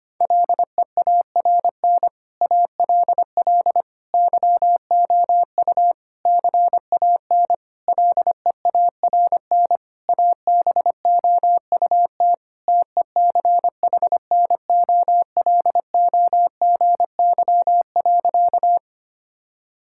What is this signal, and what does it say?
Morse code, heard as a keyed tone, about 25 wpm, 700 Hz. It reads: LEARN ALL YOU CAN LEARN ABOUT TECHNOLOGY.
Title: Sentences from Top 1000 Words - Review Part 5 - 25wpm